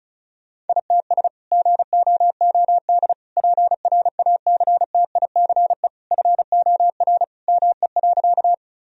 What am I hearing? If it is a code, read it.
Morse code, heard as a keyed tone, about 36 wpm, 700 Hz. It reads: ITS GOOD PRACTICE FOR ME.
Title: Sentences from Top 1000 Words - Review Part 4 - 35wpm